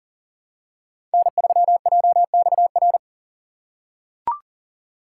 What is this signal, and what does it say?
Morse code, heard as a keyed tone, about 37 wpm, 700 Hz. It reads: N3JXR
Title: USA Call Signs - 5 Characters - 2x Character Spacing - 40wpm